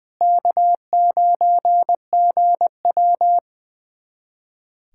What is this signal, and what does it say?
Morse code, heard as a keyed tone, about 21 wpm, 700 Hz. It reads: K9GW
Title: USA Callsign DE Callsign - 4 Characters - 20wpm